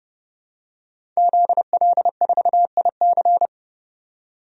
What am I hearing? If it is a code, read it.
Morse code, heard as a keyed tone, about 30 wpm, 700 Hz. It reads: ZL4IC
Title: DXCC-100 Call Signs - 5 Characters - 30wpm